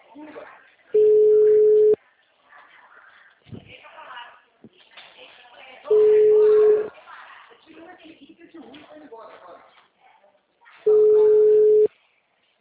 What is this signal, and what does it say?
Ringback tone (Europe, 425 Hz), with speech.